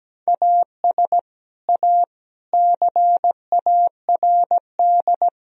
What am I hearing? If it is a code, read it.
Morse code, heard as a keyed tone, about 17 wpm, 700 Hz. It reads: AS A CARD
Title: Sentences from Top 1000 Words - Review Part 4 - 17wpm